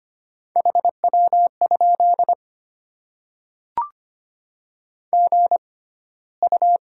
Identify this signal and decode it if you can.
Morse code, heard as a keyed tone, about 25 wpm, spaced as for 15 wpm, 700 Hz. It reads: HW? GU
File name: Top 20 QSO Elements - 6x Character Spacing - 25wpm